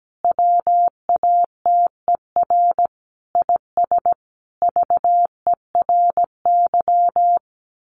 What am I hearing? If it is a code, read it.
Morse code, heard as a keyed tone, about 17 wpm, 700 Hz. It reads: WATER IS VERY
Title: Sentences from Top 1500 Words 17wpm